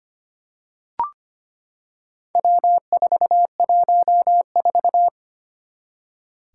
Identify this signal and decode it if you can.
Morse code, heard as a keyed tone, about 25 wpm, 700 Hz. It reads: W414
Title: Four Challenging Letter-Number Combinations 25wpm